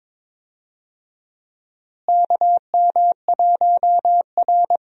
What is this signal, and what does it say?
Morse code, heard as a keyed tone, about 22 wpm, 700 Hz. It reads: KM1R
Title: USA Call Signs - 4 Characters - Repeated 2x - 22wpm